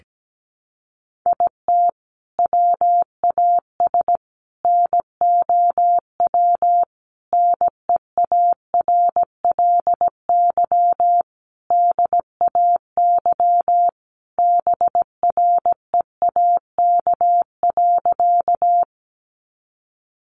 Morse code, heard as a keyed tone, about 17 wpm, 700 Hz. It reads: IT WAS NOW NEARLY DAY BREAK.
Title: Sentences from Top 1300 Words 17wpm